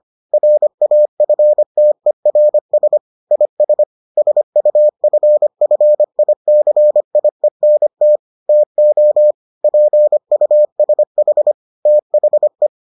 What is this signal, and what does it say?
Morse code, heard as a keyed tone, about 25 wpm, 600 Hz. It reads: RAFTERS IS SUFFICIENT TO PUSH THE